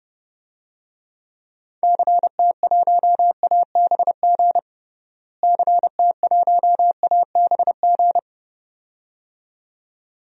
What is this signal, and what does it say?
Morse code, heard as a keyed tone, about 32 wpm, 700 Hz. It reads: CT1ABG CT1ABG
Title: DXCC-100 Call Signs - 6 Characters - Repeated 2x - 30wpm